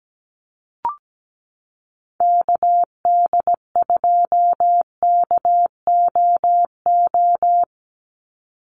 Morse code, heard as a keyed tone, about 17 wpm, 700 Hz. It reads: KD2KOO
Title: USA Call Signs 17wpm